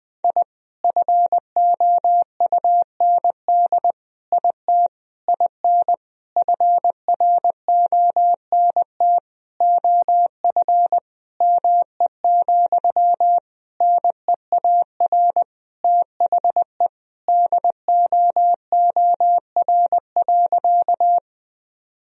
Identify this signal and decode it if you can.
Morse code, heard as a keyed tone, about 21 wpm, 700 Hz. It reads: I FOUND IT IN FRONT OF ME, NEAR THE DOOR.
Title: Sentences from Top 1000 Words - Review Part 2 - 20wpm